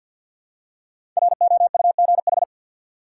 Morse code, heard as a keyed tone, about 53 wpm, 700 Hz. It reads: ROUGH
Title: Five Letter Words - 50wpm